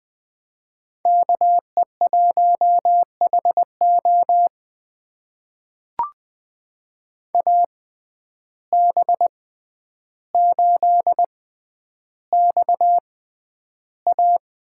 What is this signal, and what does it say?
Morse code, heard as a keyed tone, about 20 wpm, spaced as for 10 wpm, 700 Hz. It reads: KE1HO AB8XA
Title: USA Call Signs - 5 Characters - 6x Character Spacing - 20wpm